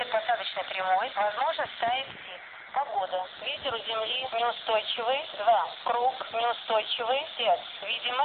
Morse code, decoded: none

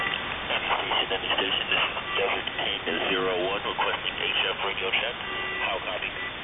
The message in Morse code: none

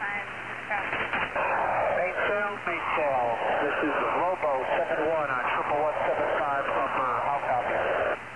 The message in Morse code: none

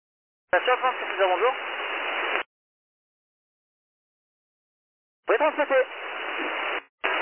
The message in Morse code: none

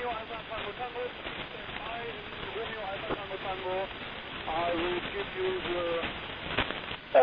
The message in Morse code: none

